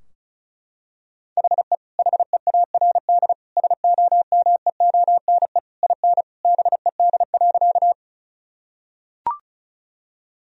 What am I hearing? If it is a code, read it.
Morse code, heard as a keyed tone, about 36 wpm, 700 Hz. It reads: HE HEARD SOMEONE IN BED.
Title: Sentences from Top 1000 Words - Review Part 4 - 35wpm